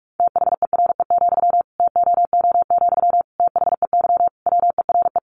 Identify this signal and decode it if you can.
Morse code, heard as a keyed tone, about 44 wpm, 700 Hz. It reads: THERE, TOO, THEY WERE
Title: Sentences from Top 1200 Words 45wpm